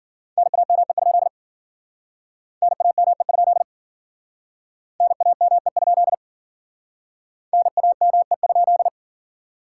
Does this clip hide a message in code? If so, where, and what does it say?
Yes: Morse code from 0.4 to 8.9 s, heard as a keyed tone, about 54 wpm, 700 Hz. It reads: NAME? NAME? NAME? NAME?